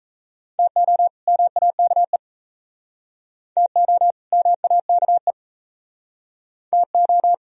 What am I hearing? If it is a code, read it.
Morse code, heard as a keyed tone, about 37 wpm, 700 Hz. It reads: TO MAKE TO MAKE TO